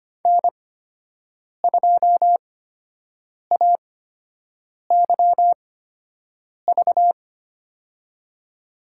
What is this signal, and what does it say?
Morse code, heard as a keyed tone, about 25 wpm, spaced as for 7 wpm, 700 Hz. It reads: N2AYV
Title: USA Call Signs - 5 Characters - 8x Character Spacing - 25wpm